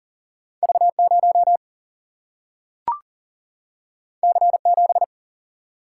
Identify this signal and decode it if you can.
Morse code, heard as a keyed tone, about 39 wpm, 700 Hz. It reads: V 0 C 7
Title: Single-Letter Single-Number 40wpm